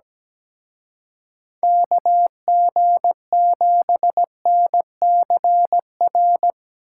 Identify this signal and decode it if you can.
Morse code, heard as a keyed tone, about 19 wpm, 700 Hz. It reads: KG7NCR
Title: Speed Racing - USA 2x3 Call Signs 17wpm